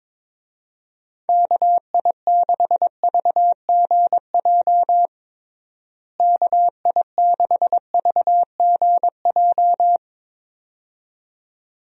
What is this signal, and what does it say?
Morse code, heard as a keyed tone, about 22 wpm, 700 Hz. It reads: KI6VGJ KI6VGJ
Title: USA Call Signs - 6 Characters - Repeated 2x - 22wpm